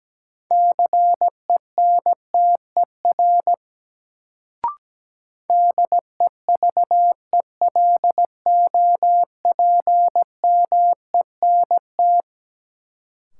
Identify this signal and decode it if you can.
Morse code, heard as a keyed tone, about 17 wpm, 700 Hz. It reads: CENTER DEVELOPMENT
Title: Sets of 3 Words - New Word Drill - 17wpm